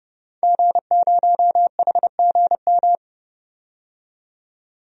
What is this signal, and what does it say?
Morse code, heard as a keyed tone, about 32 wpm, 700 Hz. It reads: G0HGM
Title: DXCC-100 Call Signs - 5 Characters - 30wpm